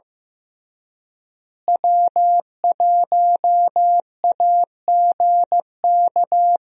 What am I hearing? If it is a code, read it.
Morse code, heard as a keyed tone, about 16 wpm, 700 Hz. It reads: W1AGK
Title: USA Call Signs - 5 Characters - Repeated 2x - 15wpm